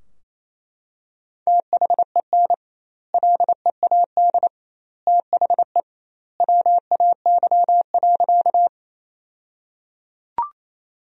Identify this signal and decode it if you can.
Morse code, heard as a keyed tone, about 28 wpm, 700 Hz. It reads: THEN LEAD THE WAY.
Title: 2x Word-Spacing - Sentences from Top 500 Words - 28wpm